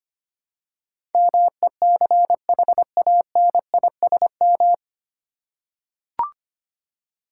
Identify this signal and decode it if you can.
Morse code, heard as a keyed tone, about 25 wpm, 700 Hz. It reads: MECHANISM
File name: Nine Letter Words - 25wpm